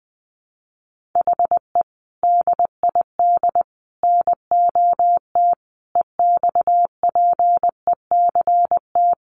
Morse code, heard as a keyed tone, about 20 wpm, 700 Hz. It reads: HE DID NOT EXPECT
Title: Sentences from Top 1100 Words 20wpm